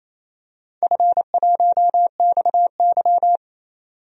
Morse code, heard as a keyed tone, about 28 wpm, 700 Hz. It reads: F1XY
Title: DXCC-100 Call Signs - 4 Characters - 28wpm